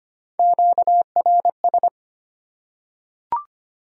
Morse code, heard as a keyed tone, about 25 wpm, 700 Hz. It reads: QRS
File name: Top 90 QSO Elements - 3x Character Spacing - 25wpm